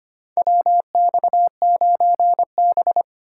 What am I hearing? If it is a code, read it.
Morse code, heard as a keyed tone, about 25 wpm, 700 Hz. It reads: WX9B
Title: USA Call Signs - 4 Characters - 4x Character Spacing - 25wpm